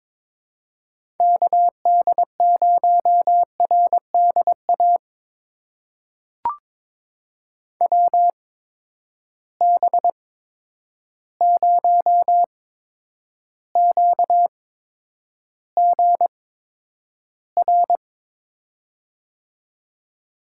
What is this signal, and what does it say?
Morse code, heard as a keyed tone, about 22 wpm, spaced as for 9 wpm, 700 Hz. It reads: KD0RDA WB0QGR
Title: USA Call Signs - 6 Characters - 8x Character Spacing - 22wpm